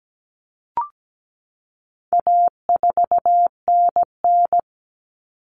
Morse code, heard as a keyed tone, about 17 wpm, 700 Hz. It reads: A4NN
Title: Common Cut Numbers - 4-Digit 17wpm